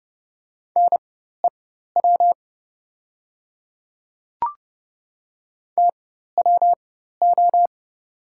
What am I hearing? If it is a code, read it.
Morse code, heard as a keyed tone, about 30 wpm, spaced as for 14 wpm, 700 Hz. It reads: NEW TWO
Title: 4x Character Spacing - Top 300 Words - 30wpm